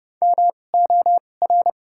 Morse code, heard as a keyed tone, about 29 wpm, spaced as for 21 wpm, 700 Hz. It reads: MOR
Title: 2x Character Spacing - Top 100 Words - 30wpm